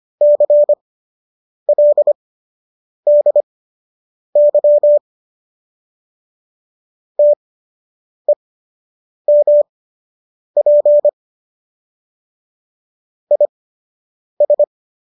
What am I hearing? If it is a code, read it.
Morse code, heard as a keyed tone, about 25 wpm, spaced as for 8 wpm, 590 Hz. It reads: CLDY TEMP IS